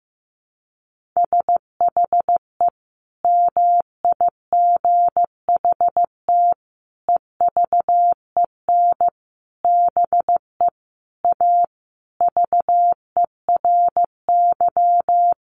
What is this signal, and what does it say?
Morse code, heard as a keyed tone, about 15 wpm, 700 Hz. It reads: SHE MIGHT EVEN BE A VERY
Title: Sentences from Top 1500 Words 15wpm